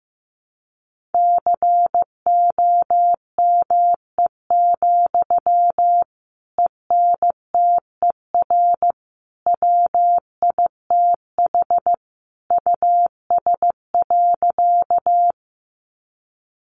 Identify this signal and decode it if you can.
Morse code, heard as a keyed tone, about 15 wpm, 700 Hz. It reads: COME, ENTER WITH US.
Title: Sentences from Top 1300 Words 15wpm